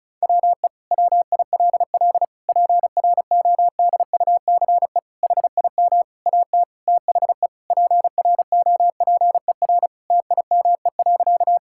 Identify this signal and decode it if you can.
Morse code, heard as a keyed tone, about 34 wpm, 700 Hz. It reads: WE WILL PRODUCE HIM AT THE PROPER TIME.
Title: Sentences from Top 1000 Words - Review Part 5 - 35wpm